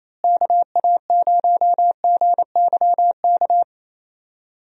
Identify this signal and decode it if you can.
Morse code, heard as a keyed tone, about 28 wpm, 700 Hz. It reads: KA0GYK
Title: USA Call Signs - 6 Characters - Repeated 2x - 28wpm